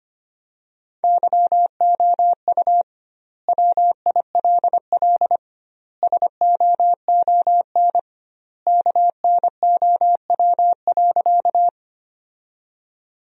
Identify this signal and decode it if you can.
Morse code, heard as a keyed tone, about 25 wpm, 700 Hz. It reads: YOU WILL SOON KNOW.
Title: 2x Word-Spacing - Sentences from Top 400 Words - 25wpm